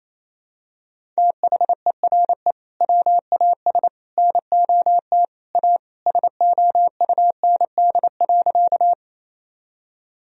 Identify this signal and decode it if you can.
Morse code, heard as a keyed tone, about 28 wpm, 700 Hz. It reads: THERE WAS NOT A SOUND.
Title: Sentences from Top 1000 Words - Review Part 1 - 28wpm